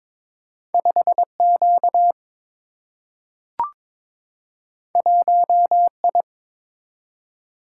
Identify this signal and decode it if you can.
Morse code, heard as a keyed tone, about 21 wpm, 700 Hz. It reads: 5 Q 1 I
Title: Single-Letter Single-Number 22wpm